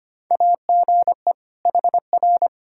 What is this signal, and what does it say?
Morse code, heard as a keyed tone, about 25 wpm, 700 Hz. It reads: AGE HR
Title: Practice QSO - Exchange 3 - More Info - 25wpm